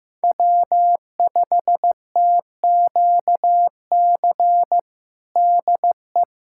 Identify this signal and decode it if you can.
Morse code, heard as a keyed tone, about 15 wpm, 700 Hz. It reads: W5TQC DE
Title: USA Callsign DE Callsign 15wpm